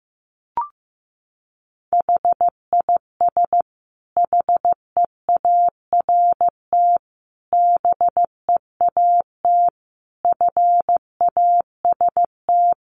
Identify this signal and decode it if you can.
Morse code, heard as a keyed tone, about 15 wpm, 700 Hz. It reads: HIS HEART BEAT FAST
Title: Sentences from Top 1300 Words 15wpm